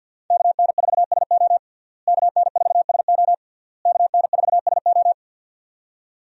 Morse code, heard as a keyed tone, about 52 wpm, 700 Hz. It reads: KN4SO KN4SO KN4SO
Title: USA Call Signs - 5 Characters - Repeated 3x - 50wpm